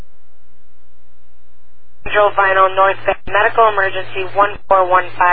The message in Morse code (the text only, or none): none